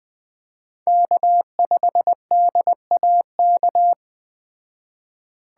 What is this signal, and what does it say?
Morse code, heard as a keyed tone, about 20 wpm, 700 Hz. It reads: K5DAK